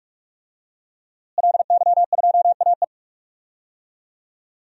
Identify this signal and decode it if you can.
Morse code, heard as a keyed tone, about 46 wpm, 700 Hz. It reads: LY2AE
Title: DXCC-100 Call Signs - 5 Characters - Repeated 2x - 45wpm